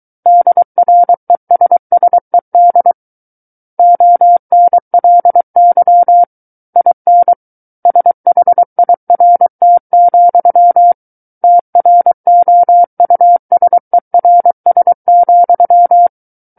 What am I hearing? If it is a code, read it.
Morse code, heard as a keyed tone, about 23 wpm, 700 Hz. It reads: DRESSED ONLY IN SHIRT, TROUSERS,
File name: Book One / 14-Chapter 14 - 23 WPM